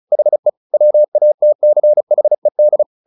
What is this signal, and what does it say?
Morse code, heard as a keyed tone, about 34 wpm, 600 Hz. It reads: HE WATCHED
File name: the old man and the sea / the old man and the sea 1 35wpm